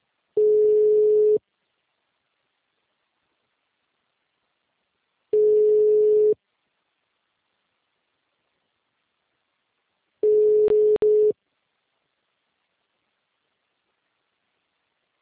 Ringback tone (Europe, 425 Hz).